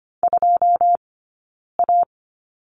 Morse code, heard as a keyed tone, about 25 wpm, 700 Hz. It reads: 2 A
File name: course / 25:10wpm / 092 Lesson 16d - Letter c - Two Characters 25wpm